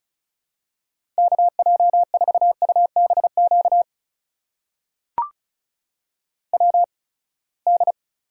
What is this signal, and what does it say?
Morse code, heard as a keyed tone, about 36 wpm, spaced as for 24 wpm, 700 Hz. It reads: KJ4UBQ WD
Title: USA Call Signs - 6 Characters - 8x Character Spacing - 35wpm